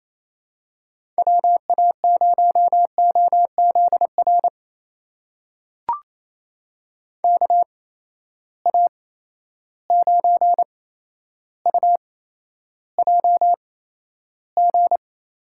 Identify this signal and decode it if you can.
Morse code, heard as a keyed tone, about 28 wpm, spaced as for 12 wpm, 700 Hz. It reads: WA0OZR KA9UJG
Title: USA Call Signs - 6 Characters - 8x Character Spacing - 28wpm